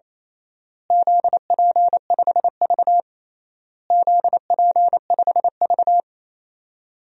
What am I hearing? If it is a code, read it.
Morse code, heard as a keyed tone, about 28 wpm, 700 Hz. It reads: ZP5V ZP5V
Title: DXCC-100 Call Signs - 4 Characters - Repeated 3x - 28wpm